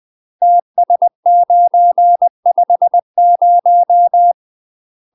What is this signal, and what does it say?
Morse code, heard as a keyed tone, about 20 wpm, 700 Hz. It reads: TS950
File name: UK MASTER QSOs TRAINING / 20WPM / UK MASTER QSOs 0046